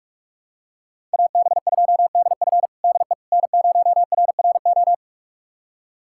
Morse code, heard as a keyed tone, about 46 wpm, 700 Hz. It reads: AB2DF DE N0RRO
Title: USA Callsign DE Callsign 45wpm